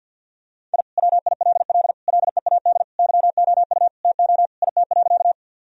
Morse code, heard as a keyed tone, about 50 wpm, 700 Hz. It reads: I WILL LEAD YOU TO IT.
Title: Sentences from Top 1000 Words - Review Part 2 - 50wpm